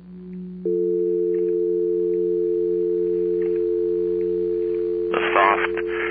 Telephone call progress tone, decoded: dial tone